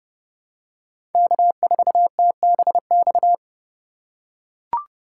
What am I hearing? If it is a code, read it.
Morse code, heard as a keyed tone, about 30 wpm, 700 Hz. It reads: K4TBX
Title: USA Call Signs - 5 Characters - 3x Character Spacing - 30wpm